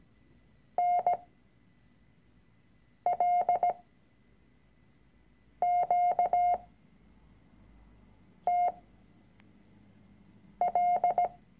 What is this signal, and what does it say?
Morse code, heard as a keyed tone, about 17 wpm, spaced as for 4 wpm, 710 Hz. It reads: NLQTL
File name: CW-Online-Kurs / Anfänger / Lektion 006